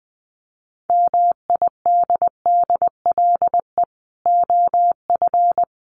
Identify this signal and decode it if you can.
Morse code, heard as a keyed tone, about 20 wpm, 700 Hz. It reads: MIDDLE OF
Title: Sets of 4 Words 20wpm